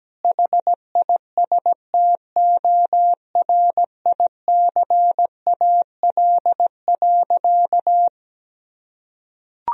Morse code, heard as a keyed tone, about 17 wpm, 700 Hz. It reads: HISTORICAL.
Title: Sentences from Top 1000 Words - Review Part 4 - 17wpm